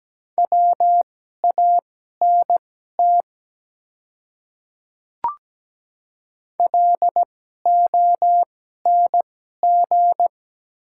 Morse code, heard as a keyed tone, about 17 wpm, spaced as for 12 wpm, 700 Hz. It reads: WANT LONG
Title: 2x Character Spacing - Top 200 Words - 17wpm